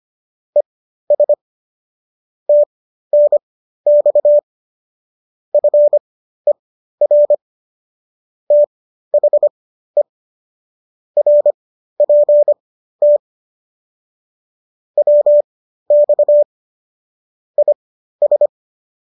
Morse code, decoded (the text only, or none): ES TNX FER THE RPT WX IS